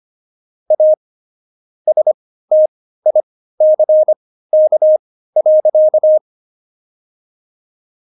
Morse code, heard as a keyed tone, about 25 wpm, spaced as for 15 wpm, 620 Hz. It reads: A STICK.
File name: Jokes-25-char-15-eff